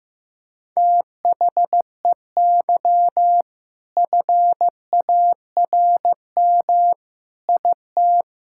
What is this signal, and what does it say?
Morse code, heard as a keyed tone, about 15 wpm, 690 Hz. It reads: THEY FARM IT
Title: Sentences from Top 1000 Words - Review Part 5 - 15wpm